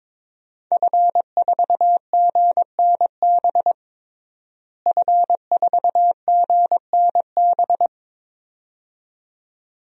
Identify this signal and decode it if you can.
Morse code, heard as a keyed tone, about 22 wpm, 700 Hz. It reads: F4GNB F4GNB
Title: DXCC-100 Call Signs - 5 Characters - Repeated 2x - 22wpm